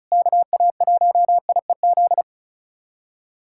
Morse code, heard as a keyed tone, about 36 wpm, 700 Hz. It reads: KA1IEZ
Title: SOTA QSO Practice 35wpm